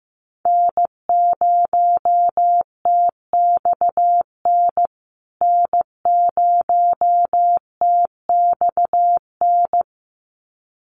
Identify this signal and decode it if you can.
Morse code, heard as a keyed tone, about 15 wpm, 700 Hz. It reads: N0TXN N0TXN
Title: USA Call Signs Repeated 2x 15wpm